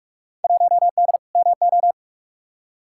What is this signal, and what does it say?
Morse code, heard as a keyed tone, about 44 wpm, 700 Hz. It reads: 1D MO
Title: Field Day Exchanges - 45wpm